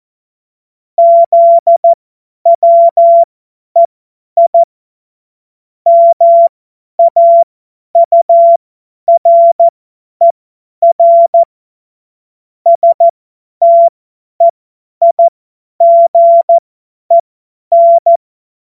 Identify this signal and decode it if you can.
Morse code, heard as a keyed tone, about 14 wpm, spaced as for 10 wpm, 680 Hz. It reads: ZWEI MAURER STEIGEN
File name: Morsekurs / Witze / Witz 16. Dezember 2025 50 F70